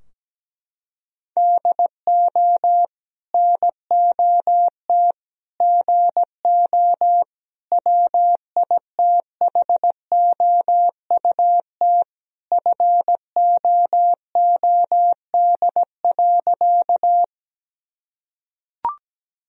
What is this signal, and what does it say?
Morse code, heard as a keyed tone, about 17 wpm, 700 Hz. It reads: DO NOT GO WITHOUT FOOD.
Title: Sentences from Top 1000 Words - Review Part 3 - 17wpm